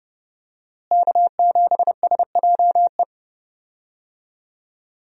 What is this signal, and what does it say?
Morse code, heard as a keyed tone, about 30 wpm, 700 Hz. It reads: K7SJE
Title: USA Call Signs - 5 Characters - Repeated 2x - 30wpm